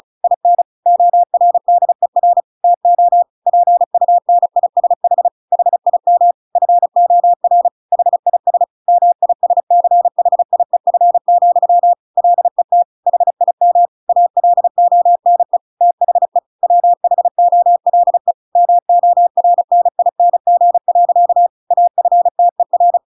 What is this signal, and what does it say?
Morse code, heard as a keyed tone, about 35 wpm, 700 Hz. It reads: IN ORDER TO PUNISH HIM FOR HIS MISCHIEF, LET HIM ALONE THE WHOLE MORNING. AFTER
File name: Pinocchio / Pinocchio ch8 35 35 ik1hge